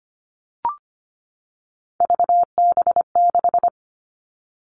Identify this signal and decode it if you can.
Morse code, heard as a keyed tone, about 25 wpm, 700 Hz. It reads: VB6
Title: All Cut Numbers - 3-Digit 25wpm